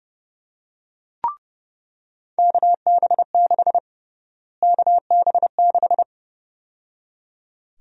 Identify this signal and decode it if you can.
Morse code, heard as a keyed tone, about 30 wpm, 700 Hz. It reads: KB6 KB6
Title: USA Call Sign Prefixes - Repeated 2x - 30wpm